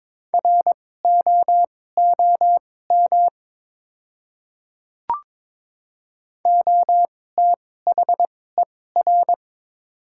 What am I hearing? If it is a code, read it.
Morse code, heard as a keyed tone, about 22 wpm, spaced as for 16 wpm, 690 Hz. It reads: ROOM OTHER
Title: 2x Character Spacing - Top 300 Words - 22wpm